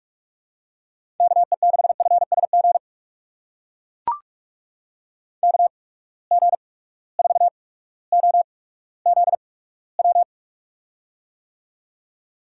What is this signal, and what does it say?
Morse code, heard as a keyed tone, about 43 wpm, spaced as for 19 wpm, 700 Hz. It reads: KE6FSG KG4OZW
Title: USA Call Signs - 6 Characters - 8x Character Spacing - 45wpm